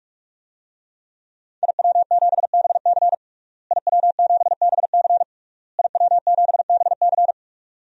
Morse code, heard as a keyed tone, about 46 wpm, 700 Hz. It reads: IW7BC IW7BC IW7BC
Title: DXCC-100 Call Signs - 5 Characters - Repeated 3x - 45wpm